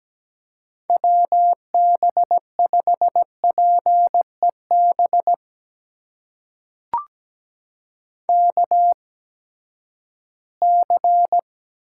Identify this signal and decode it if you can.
Morse code, heard as a keyed tone, about 17 wpm, spaced as for 12 wpm, 700 Hz. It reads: WB5PEB KC
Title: USA Call Signs - 6 Characters - 8x Character Spacing - 17wpm